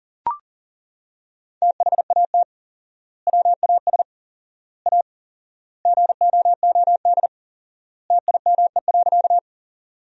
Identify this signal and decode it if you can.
Morse code, heard as a keyed tone, about 39 wpm, 700 Hz. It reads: THAT WAS A GOOD TIME.
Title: 4x Word-Spacing - Sentences from Top 200 Words - 40wpm